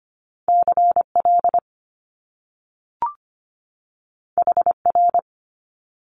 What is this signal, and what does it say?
Morse code, heard as a keyed tone, about 25 wpm, 700 Hz. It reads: CL HR
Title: Top Two Letter Combinations in Words 25wpm